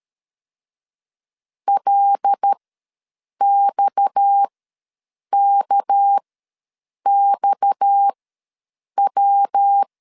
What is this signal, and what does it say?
Morse code, heard as a keyed tone, about 13 wpm, spaced as for 7 wpm, 800 Hz. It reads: LXKXW
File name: LEZIONI / 6 PKX0 / 6 PKX ws 12wpm 3106